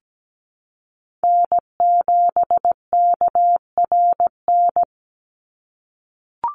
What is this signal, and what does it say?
Morse code, heard as a keyed tone, about 17 wpm, 700 Hz. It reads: N7KRN
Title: NA SOTA Chaser Callsigns 17wpm